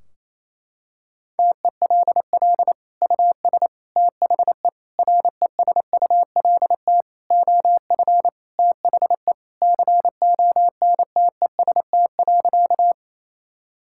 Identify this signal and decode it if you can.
Morse code, heard as a keyed tone, about 28 wpm, 700 Hz. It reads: TELL US THE RESULT OF THE CONTEST.
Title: Sentences from Top 1600 Words 28wpm